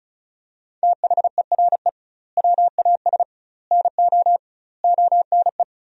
Morse code, heard as a keyed tone, about 36 wpm, 700 Hz. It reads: THERE WAS NO ONE
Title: 2x Word-Spacing - Sentences from Top 200 Words - 35wpm